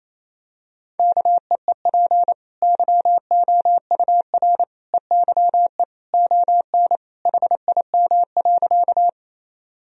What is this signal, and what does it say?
Morse code, heard as a keyed tone, about 28 wpm, 700 Hz. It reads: KEEP YOUR EYE ON HIM.
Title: Sentences from Top 1000 Words - Review Part 1 - 28wpm